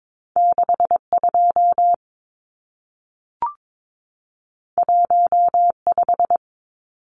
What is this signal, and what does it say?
Morse code, heard as a keyed tone, about 22 wpm, 700 Hz. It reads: 62 15
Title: Numbers - 2-Digit 22wpm